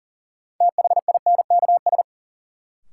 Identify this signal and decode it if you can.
Morse code, heard as a keyed tone, about 40 wpm, 700 Hz. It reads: THINKS
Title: Sentences from Top 1000 Words - Review Part 6 - New Word Drill - 40wpm